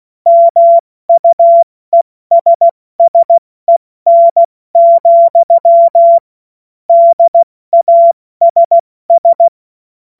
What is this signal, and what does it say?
Morse code, heard as a keyed tone, about 16 wpm, 680 Hz. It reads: MUESSEN, DASS
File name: Morsekurs / Witze / Witz 27. Juni 2023 70 F80